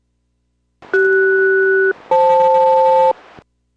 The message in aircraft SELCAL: CS-FL